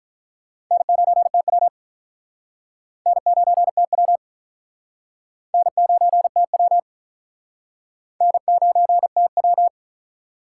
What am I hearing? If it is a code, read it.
Morse code, heard as a keyed tone, about 44 wpm, 700 Hz. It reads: N9TW N9TW N9TW N9TW
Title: Speed Racing - USA 1x2 Call Signs 35wpm